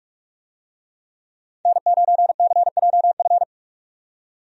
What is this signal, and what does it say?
Morse code, heard as a keyed tone, about 46 wpm, 700 Hz. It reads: N9CJF
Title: USA Call Signs - 5 Characters - 45wpm